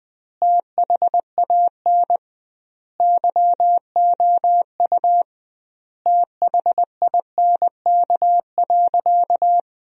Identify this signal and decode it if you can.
Morse code, heard as a keyed tone, about 20 wpm, 700 Hz. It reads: THAN YOU THINK.
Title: 2x Word-Spacing - Sentences from Top 400 Words - 20wpm